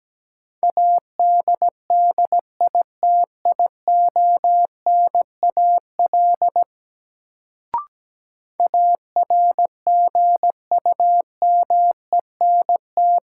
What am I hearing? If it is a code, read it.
Morse code, heard as a keyed tone, about 17 wpm, 700 Hz. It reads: ADDITIONAL ARGUMENT
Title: Sentences from Top 1400 Words - New Word Drill Alt - 17wpm